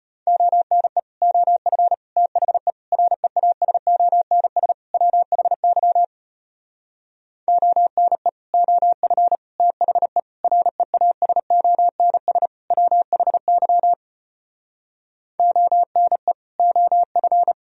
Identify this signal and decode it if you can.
Morse code, heard as a keyed tone, about 33 wpm, 700 Hz. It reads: ONE OF THE REASONS WHY ONE OF THE REASONS WHY ONE OF